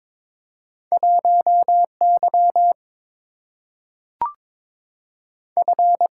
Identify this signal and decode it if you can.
Morse code, heard as a keyed tone, about 22 wpm, 700 Hz. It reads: 1Y F
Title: Single-Letter Single-Number 22wpm